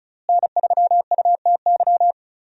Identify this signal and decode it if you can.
Morse code, heard as a keyed tone, about 34 wpm, 700 Hz. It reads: N3UTY
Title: USA Call Signs - 5 Characters - Repeated 2x - 35wpm